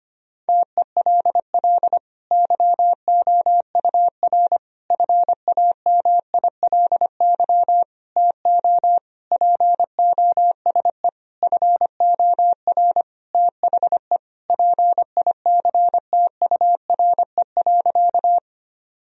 Morse code, heard as a keyed tone, about 25 wpm, 700 Hz. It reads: TELL YOUR FAMILY TO POSE FOR THE PICTURE.
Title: Sentences from Top 1000 Words - Review Part 2 - 25wpm